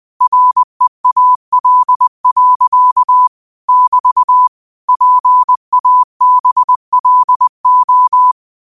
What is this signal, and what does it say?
Morse code, heard as a keyed tone, about 20 wpm, 1000 Hz. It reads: REAL. <BT> PABLO
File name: qotdfiles / 20wpm